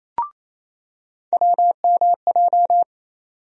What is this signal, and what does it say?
Morse code, heard as a keyed tone, about 28 wpm, 700 Hz. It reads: WMJ
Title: USA Call Sign Postfixes - 3 Letters 28wpm